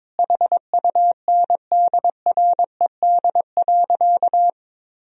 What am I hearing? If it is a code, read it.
Morse code, heard as a keyed tone, about 22 wpm, 700 Hz. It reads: HUNDRED.
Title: Sentences from Top 1000 Words - Review Part 2 - 22wpm